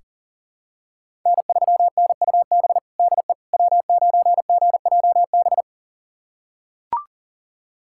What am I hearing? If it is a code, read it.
Morse code, heard as a keyed tone, about 40 wpm, 700 Hz. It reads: N3NUB DE W9GJB
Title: USA Callsign DE Callsign - 5 Characters - 40wpm